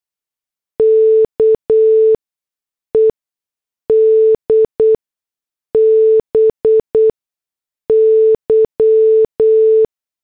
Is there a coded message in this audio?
Yes: Morse code from 0.8 to 9.8 s, heard as a keyed tone, about 8 wpm, spaced as for 6 wpm, 440 Hz. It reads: KED BY